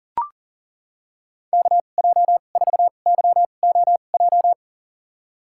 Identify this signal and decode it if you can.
Morse code, heard as a keyed tone, about 38 wpm, spaced as for 28 wpm, 700 Hz. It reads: KJ4YOJ
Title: USA Call Signs - 6 Characters - 2x Character Spacing - 40wpm